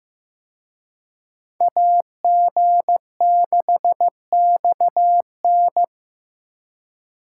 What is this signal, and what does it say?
Morse code, heard as a keyed tone, about 18 wpm, 700 Hz. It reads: AG6XN
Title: Speed Racing - USA 2x2 Call Signs 15wpm